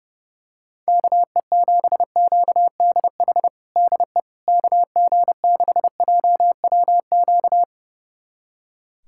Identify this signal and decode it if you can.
Morse code, heard as a keyed tone, about 30 wpm, 700 Hz. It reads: KE7QDH DE KG6JWQ
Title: USA Callsign DE Callsign - 6 Characters - 30wpm